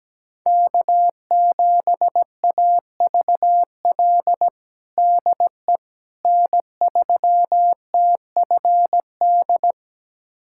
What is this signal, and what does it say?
Morse code, heard as a keyed tone, about 17 wpm, 700 Hz. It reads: K7AVL DE N3TFD
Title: USA Callsign DE Callsign - 5 Characters - 17wpm